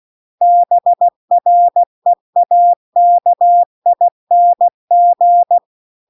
Morse code, heard as a keyed tone, about 16 wpm, 700 Hz. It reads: BREAKING
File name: CW-The-Big-Twoheaded-River-2-16WPM